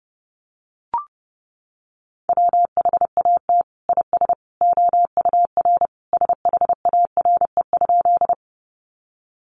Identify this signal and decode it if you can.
Morse code, heard as a keyed tone, about 30 wpm, 700 Hz. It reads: WHAT IS OUR SHARE?
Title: Sentences from Top 900 Words 30wpm